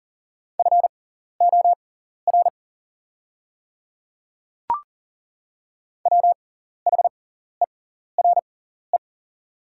Morse code, heard as a keyed tone, about 40 wpm, spaced as for 14 wpm, 690 Hz. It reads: FOR WHERE
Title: 6x Character Spacing - Top 200 Words - 40wpm